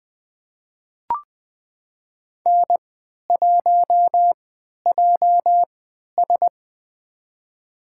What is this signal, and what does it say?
Morse code, heard as a keyed tone, about 20 wpm, spaced as for 11 wpm, 700 Hz. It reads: N1JS
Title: USA Call Signs - 3x Character Spacing - 20wpm